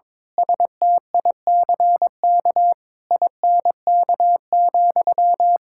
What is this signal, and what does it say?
Morse code, heard as a keyed tone, about 22 wpm, 700 Hz. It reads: STICK INK,
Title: Pangram Challenge v2 - 22wpm